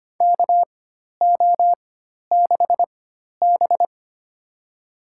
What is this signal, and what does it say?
Morse code, heard as a keyed tone, about 25 wpm, spaced as for 12 wpm, 700 Hz. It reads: KO6B
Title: USA Call Signs - 4 Characters - 4x Character Spacing - 25wpm